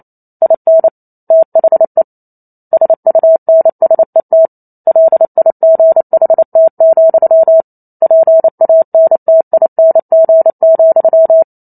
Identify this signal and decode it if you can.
Morse code, heard as a keyed tone, about 28 wpm, 650 Hz. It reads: IN THE SUNSET LIGHT, PANTING,
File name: Chapter 2 - 28 WPM